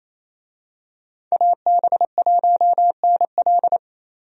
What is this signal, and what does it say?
Morse code, heard as a keyed tone, about 27 wpm, 700 Hz. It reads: AB1NL
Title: USA Call Signs - 5 Characters - 28wpm